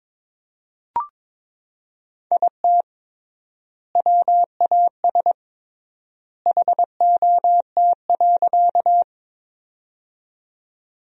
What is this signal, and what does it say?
Morse code, heard as a keyed tone, about 22 wpm, 700 Hz. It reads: IT WAS HOT.
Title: 3x Word-Spacing - Sentences from Top 500 Words - 22wpm